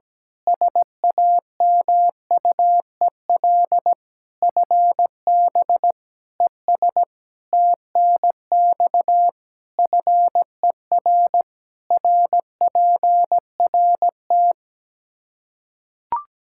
Morse code, heard as a keyed tone, about 17 wpm, 700 Hz. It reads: SAMUEL FB ES TNX FER RPRT
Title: K1USN SST QSO Practice 17wpm